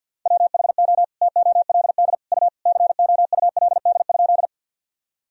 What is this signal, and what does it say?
Morse code, heard as a keyed tone, about 47 wpm, 700 Hz. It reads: WHO TOLD U COULD?
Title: Sentences from Top 400 Words and Abbreviations 50wpm